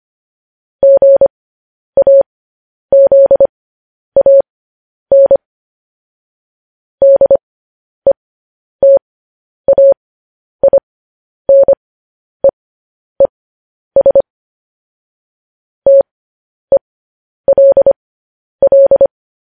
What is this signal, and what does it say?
Morse code, heard as a keyed tone, about 25 wpm, spaced as for 10 wpm, 550 Hz. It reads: GAZAN DETAINEES TELL
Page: news headlines in morse code at 10 wpm